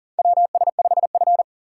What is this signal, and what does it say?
Morse code, heard as a keyed tone, about 40 wpm, 700 Hz. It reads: WS5F
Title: USA Call Signs - 4 Characters - 2x Character Spacing - 40wpm